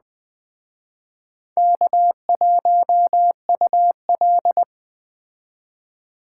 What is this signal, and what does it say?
Morse code, heard as a keyed tone, about 21 wpm, 700 Hz. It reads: K1UL